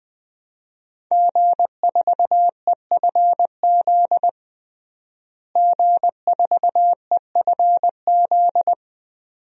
Morse code, heard as a keyed tone, about 20 wpm, 700 Hz. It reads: G4EFZ G4EFZ
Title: DXCC-100 Call Signs - 5 Characters - Repeated 2x - 20wpm